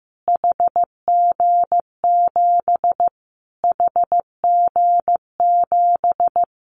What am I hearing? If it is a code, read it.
Morse code, heard as a keyed tone, about 15 wpm, 700 Hz. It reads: HG7 HG7
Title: Callsign Prefixes for Top 25 DXCC - Repeated 3x - 15wpm